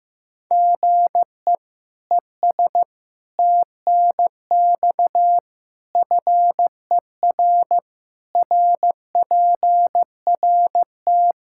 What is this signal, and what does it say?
Morse code, heard as a keyed tone, about 15 wpm, 700 Hz. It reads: GE ES TNX FER RPRT
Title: K1USN SST QSO Practice 15wpm